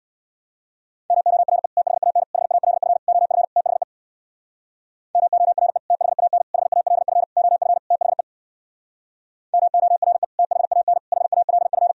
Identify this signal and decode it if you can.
Morse code, heard as a keyed tone, about 63 wpm, 700 Hz. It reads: MORE THAN HALF OF THE MORE THAN HALF OF THE MORE THAN HALF